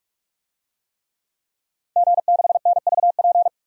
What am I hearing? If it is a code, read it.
Morse code, heard as a keyed tone, about 46 wpm, 700 Hz. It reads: G6NVP
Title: DXCC-100 Call Signs - 5 Characters - 45wpm